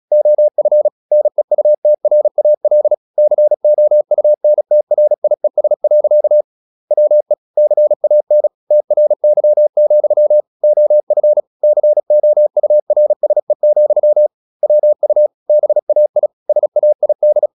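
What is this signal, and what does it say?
Morse code, heard as a keyed tone, about 36 wpm, 600 Hz. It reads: OF NEUTRAL COUNTRIES. WE CAN TRY, OF COURSE, WU BAI SAID
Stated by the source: Morse practice at 36 wpm, text of Ken Follett's novel Never, from CW med Gnister.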